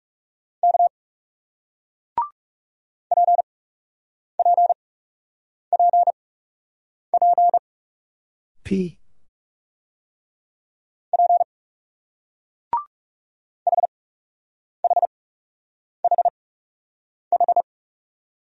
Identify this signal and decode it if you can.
Morse code, heard as a keyed tone, about 36 wpm, spaced as for 8 wpm, 700 Hz. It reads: K PPPP P HHHH
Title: Speed-Racing Single Letters 30wpm